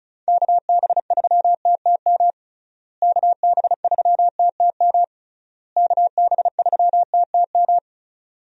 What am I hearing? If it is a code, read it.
Morse code, heard as a keyed tone, about 34 wpm, 700 Hz. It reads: KB3TTM KB3TTM KB3TTM
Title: USA Call Signs - 6 Characters - Repeated 3x - 35wpm